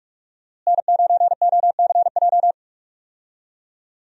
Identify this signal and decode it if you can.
Morse code, heard as a keyed tone, about 42 wpm, 700 Hz. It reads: N9OCJ
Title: USA Call Signs - 5 Characters - Repeated 3x - 45wpm